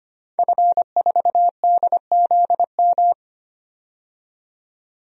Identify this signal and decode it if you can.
Morse code, heard as a keyed tone, about 25 wpm, 700 Hz. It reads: F4DZM
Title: DXCC-100 Call Signs - 5 Characters - 25wpm